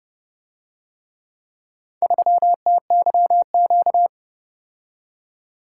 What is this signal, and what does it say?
Morse code, heard as a keyed tone, about 29 wpm, 700 Hz. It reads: 3TYQ